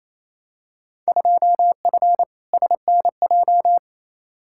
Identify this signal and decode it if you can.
Morse code, heard as a keyed tone, about 28 wpm, 700 Hz. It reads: 2F SNJ
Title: Field Day Exchanges Encore - 28wpm